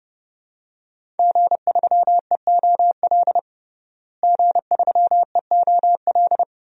Morse code, heard as a keyed tone, about 30 wpm, 700 Hz. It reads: G3EOL G3EOL
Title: DXCC-100 Call Signs - 5 Characters - Repeated 3x - 30wpm